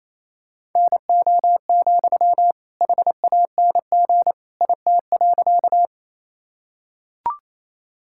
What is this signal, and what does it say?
Morse code, heard as a keyed tone, about 28 wpm, 700 Hz. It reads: NO, HANG IT.
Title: Sentences from Top 1600 Words 28wpm